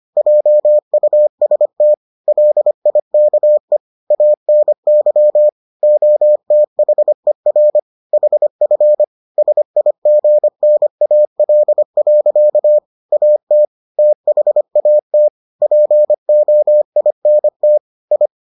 Morse code, heard as a keyed tone, about 25 wpm, 600 Hz. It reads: JUST LIKE ANY OTHER HF SIGNAL. AT THAT POINT I